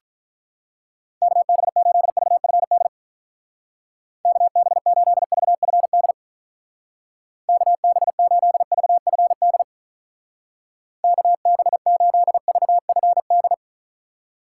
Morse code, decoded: KB8VFD KB8VFD KB8VFD KB8VFD

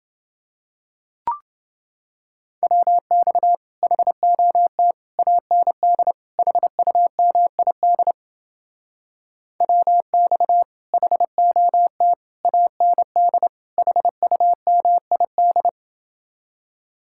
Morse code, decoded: WX HOT AND HUMID WX HOT AND HUMID

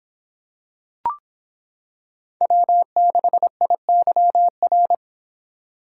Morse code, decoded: W6IYR

